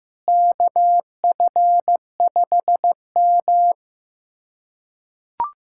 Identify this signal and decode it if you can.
Morse code, heard as a keyed tone, about 15 wpm, 700 Hz. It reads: KF5M
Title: USA Callsign DE Callsign - 4 Characters - 15wpm